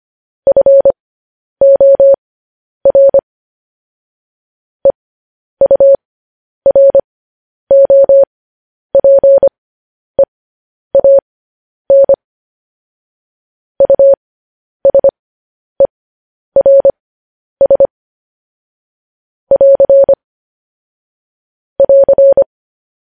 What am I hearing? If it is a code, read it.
Morse code, heard as a keyed tone, about 25 wpm, spaced as for 10 wpm, 550 Hz. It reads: FOR EUROPEAN USERS <AR> <AR>
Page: News Headlines in Morse Code at 10 WPM